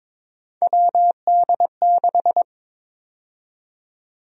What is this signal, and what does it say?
Morse code, heard as a keyed tone, about 22 wpm, 700 Hz. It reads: WD6
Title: USA Call Sign Prefixes - 22wpm